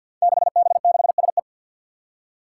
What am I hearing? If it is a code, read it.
Morse code, heard as a keyed tone, about 49 wpm, 700 Hz. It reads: 6B6SE
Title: Five Challenging Letter-Number Combinations 50wpm